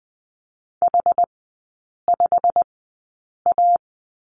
Morse code, heard as a keyed tone, about 20 wpm, spaced as for 8 wpm, 700 Hz. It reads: H5A
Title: 127 Lesson 21e - Number 6 - Three Characters 20wpm